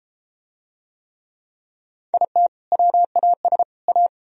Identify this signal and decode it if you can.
Morse code, heard as a keyed tone, about 28 wpm, 700 Hz. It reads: IT WAS A